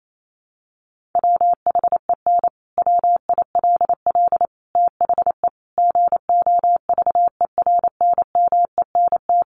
Morse code, heard as a keyed tone, about 28 wpm, 700 Hz. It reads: WHEN WILL THE GOVERNMENT
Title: Sentences from Top 1400 Words 28wpm